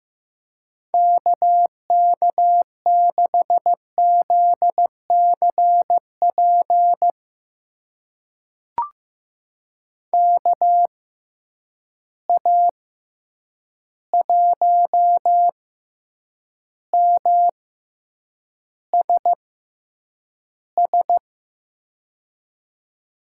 Morse code, heard as a keyed tone, about 15 wpm, spaced as for 8 wpm, 700 Hz. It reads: KK6ZCP KA1MSS
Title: USA Call Signs - 6 Characters - 6x Character Spacing - 15wpm